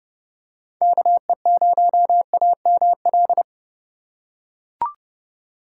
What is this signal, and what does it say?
Morse code, heard as a keyed tone, about 31 wpm, 700 Hz. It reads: KE0AML